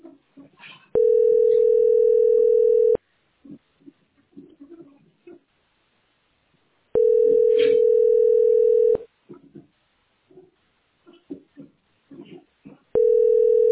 Ringback tone (North America), 440 + 480 Hz.